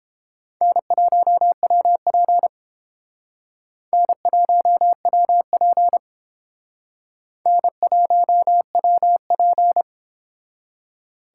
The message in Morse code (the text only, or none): N1WP N1WP N1WP